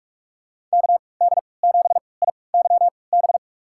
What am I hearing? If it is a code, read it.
Morse code, heard as a keyed tone, about 47 wpm, spaced as for 26 wpm, 700 Hz. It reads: KD7IYB
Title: USA Call Signs - 3x Character Spacing - 45wpm